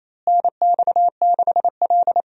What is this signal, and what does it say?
Morse code, heard as a keyed tone, about 27 wpm, 700 Hz. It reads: NX6L
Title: USA Call Signs - 4 Characters - Repeated 2x - 28wpm